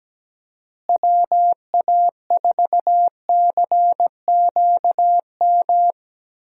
Morse code, heard as a keyed tone, about 17 wpm, 700 Hz. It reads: WA4CQM
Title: USA Call Signs - 6 Characters - Repeated 2x - 17wpm